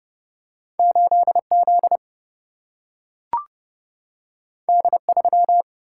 Morse code, heard as a keyed tone, about 31 wpm, 700 Hz. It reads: 8 Z D 3
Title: Single-Letter Single-Number 30wpm